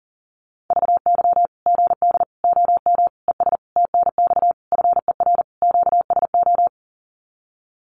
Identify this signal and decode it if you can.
Morse code, heard as a keyed tone, about 40 wpm, 700 Hz. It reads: VY GD OM ES TNX FER QSO
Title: Common QSO Phrases 40wpm